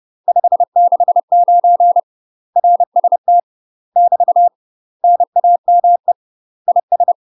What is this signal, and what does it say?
Morse code, heard as a keyed tone, about 30 wpm, 700 Hz. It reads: 569 RST <BT> NAME IS
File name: UK MASTER QSOs TRAINING / 30WPM / UK MASTER QSOs 0008